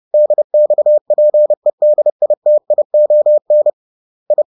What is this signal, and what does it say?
Morse code, heard as a keyed tone, about 30 wpm, 600 Hz. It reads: DXPEDITION I
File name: 30wpm-7